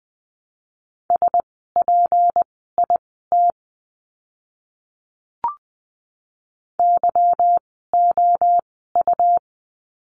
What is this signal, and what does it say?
Morse code, heard as a keyed tone, about 20 wpm, spaced as for 15 wpm, 700 Hz. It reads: SPIT YOU
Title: 153 Lesson 25f - Letter p - Words 20wpm